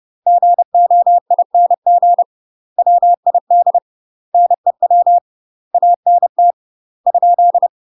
Morse code, heard as a keyed tone, about 30 wpm, 700 Hz. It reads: GOING WID NEW ANT ?
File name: UK MASTER QSOs TRAINING / 30WPM / UK MASTER QSOs 0011